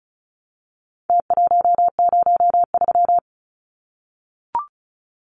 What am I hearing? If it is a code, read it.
Morse code, heard as a keyed tone, about 34 wpm, 700 Hz. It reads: T103
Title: Numbers - Contest Style - 4-Digit 35wpm